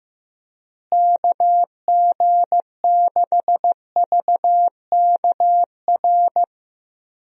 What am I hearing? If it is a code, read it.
Morse code, heard as a keyed tone, about 16 wpm, 700 Hz. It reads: KG6VKR
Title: USA Call Signs - 6 Characters - Repeated 3x - 15wpm